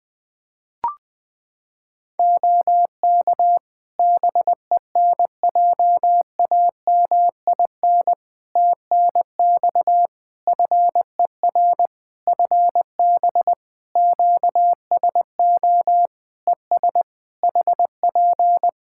Morse code, heard as a keyed tone, about 21 wpm, 700 Hz. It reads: OK BENJAMIN TNX FER FB QSO ES HP
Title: K1USN SST QSO Practice 20wpm